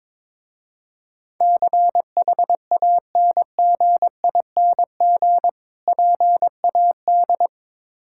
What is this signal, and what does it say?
Morse code, heard as a keyed tone, about 22 wpm, 700 Hz. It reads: CHANGING PAD
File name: Difficult Words with Hints - 22wpm